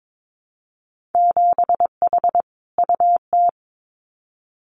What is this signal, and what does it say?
Morse code, heard as a keyed tone, about 22 wpm, 700 Hz. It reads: 7H UT
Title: Winter Field Day Exchanges Encore 22wpm